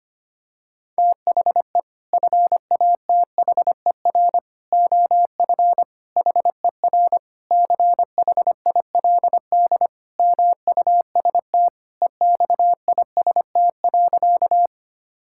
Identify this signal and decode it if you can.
Morse code, heard as a keyed tone, about 25 wpm, 700 Hz. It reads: THE FATHER OF HER CHILD MUST EXIST.
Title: Sentences from Top 1000 Words - Review Part 5 - 25wpm